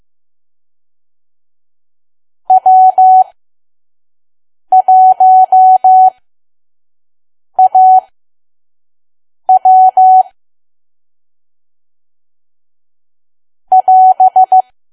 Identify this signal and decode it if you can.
Morse code, heard as a keyed tone, about 15 wpm, spaced as for 5 wpm, 750 Hz. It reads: W1AW <AS>
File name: Archive / 5 WPM files / PM